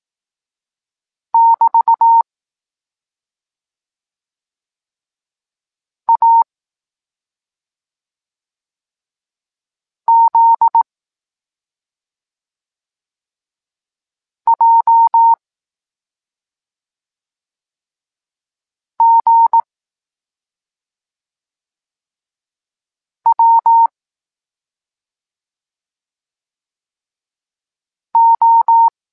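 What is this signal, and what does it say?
Morse code, heard as a keyed tone, about 18 wpm, 920 Hz. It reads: <BT> A Z J G W O
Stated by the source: Evenly spaced Morse lesson text 21 for learning letters G and Z, from SP9MOA Niepołomicki Klub Krótkofalowców.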